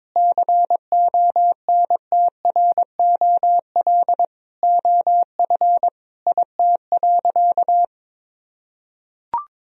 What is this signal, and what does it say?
Morse code, heard as a keyed tone, about 22 wpm, 700 Hz. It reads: CONTROL OF IT.
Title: Sentences from Top 1000 Words - Review Part 3 - 22wpm